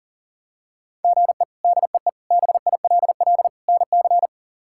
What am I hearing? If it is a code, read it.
Morse code, heard as a keyed tone, about 40 wpm, 700 Hz. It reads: GE DEE BILL NC
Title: K1USN SST QSO Practice - 40wpm